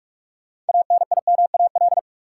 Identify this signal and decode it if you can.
Morse code, heard as a keyed tone, about 46 wpm, 700 Hz. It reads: ANIMAL
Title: Sentences from Top 1000 Words - Review Part 1 - New Word Drill - 45wpm